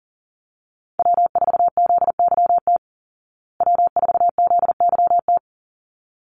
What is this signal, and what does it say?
Morse code, heard as a keyed tone, about 40 wpm, 700 Hz. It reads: W4ZYT W4ZYT
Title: Call Signs Repeated 3x - 40wpm